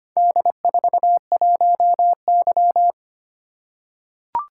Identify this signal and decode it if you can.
Morse code, heard as a keyed tone, about 25 wpm, 700 Hz. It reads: D41Y
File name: practice / 25wpm / Four Letter-Number Combinations - 25wpm